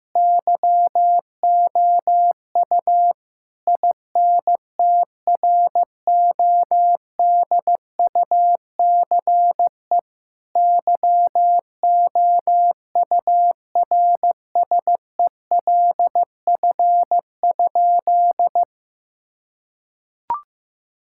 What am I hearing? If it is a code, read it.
Morse code, heard as a keyed tone, about 15 wpm, 700 Hz. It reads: YOU INTRODUCE YOURSELF?
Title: Sentences from Top 1600 Words 15wpm